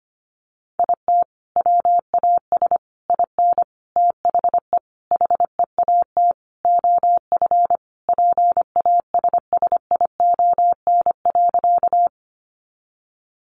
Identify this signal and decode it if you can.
Morse code, heard as a keyed tone, about 25 wpm, 700 Hz. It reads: IT WAS IN THE HEAT OF PASSION.
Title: Sentences from Top 1300 Words 25wpm